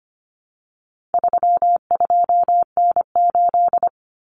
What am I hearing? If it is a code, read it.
Morse code, heard as a keyed tone, about 25 wpm, 700 Hz. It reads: 32N8
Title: Common Cut Numbers - 4-Digit 25wpm